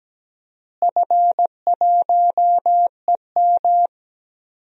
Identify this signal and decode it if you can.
Morse code, heard as a keyed tone, about 17 wpm, 700 Hz. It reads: F1EM
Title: DXCC-100 Call Signs - 4 Characters - Repeated 3x - 17wpm